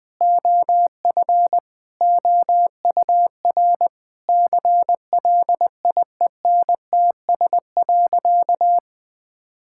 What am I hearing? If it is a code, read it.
Morse code, heard as a keyed tone, about 20 wpm, 700 Hz. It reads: OF OUR CLIENTS.
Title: Sentences with 5-Word Phrases - Spoken Beginning - 20wpm